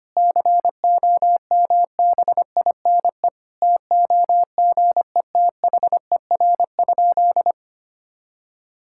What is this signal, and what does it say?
Morse code, heard as a keyed tone, about 25 wpm, 700 Hz. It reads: COMBINE TOGETHER?
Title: Sentences from Top 1600 Words 25wpm